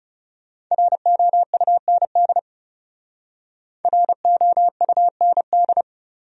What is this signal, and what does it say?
Morse code, heard as a keyed tone, about 39 wpm, 700 Hz. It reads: ROUND ROUND